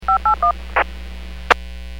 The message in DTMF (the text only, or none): 581